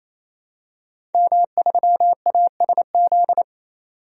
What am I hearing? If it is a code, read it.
Morse code, heard as a keyed tone, about 28 wpm, 700 Hz. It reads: M3ASZ
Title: DXCC-100 Call Signs - 5 Characters - 28wpm